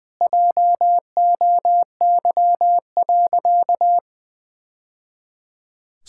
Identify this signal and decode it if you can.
Morse code, heard as a keyed tone, about 20 wpm, 700 Hz. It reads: JOY.
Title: Sentences with 2-Word Phrases - 20wpm